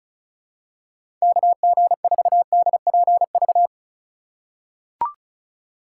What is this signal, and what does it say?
Morse code, heard as a keyed tone, about 34 wpm, 700 Hz. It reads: KG4DPV
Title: USA Call Signs - 6 Characters - 4x Character Spacing - 35wpm